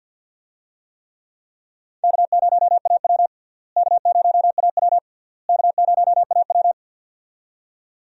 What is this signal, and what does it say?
Morse code, heard as a keyed tone, about 46 wpm, 700 Hz. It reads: K0AW K0AW K0AW
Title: USA Call Signs - 4 Characters - Repeated 3x - 50wpm